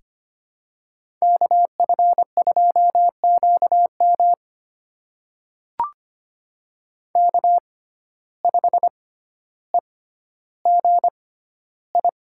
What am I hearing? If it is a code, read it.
Morse code, heard as a keyed tone, about 25 wpm, spaced as for 13 wpm, 700 Hz. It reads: KF2QM K5EGI
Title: USA Call Signs - 5 Characters - 6x Character Spacing - 25wpm